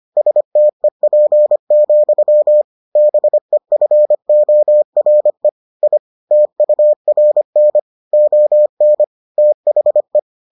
Morse code, heard as a keyed tone, about 25 wpm, 600 Hz. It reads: STEP, BEFORE I TURN ON THE